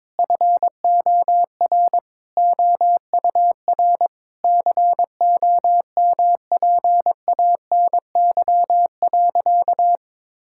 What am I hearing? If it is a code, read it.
Morse code, heard as a keyed tone, about 22 wpm, 700 Hz. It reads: FOR OUR COMPANY.
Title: Sentences from Top 1000 Words - Review Part 5 - 22wpm